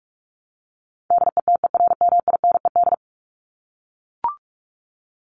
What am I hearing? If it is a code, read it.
Morse code, heard as a keyed tone, about 44 wpm, 700 Hz. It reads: DETERMINED
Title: ED Suffix - Top 100 Words with Base Words 45wpm